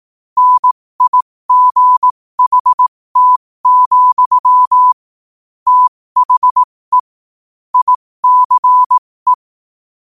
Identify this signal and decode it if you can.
Morse code, heard as a keyed tone, about 18 wpm, spaced as for 15 wpm, 1000 Hz. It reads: NIGHT, THE ICE
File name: qotdfiles / 15wpm